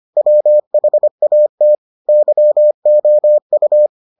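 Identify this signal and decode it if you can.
Morse code, heard as a keyed tone, about 25 wpm, 600 Hz. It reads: WHAT YOU
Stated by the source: Morse edition of the Foundations of Amateur Radio episodes.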